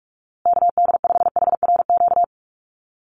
Morse code, heard as a keyed tone, about 44 wpm, 700 Hz. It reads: KD5HRQ
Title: USA Call Signs Repeated 3x 45wpm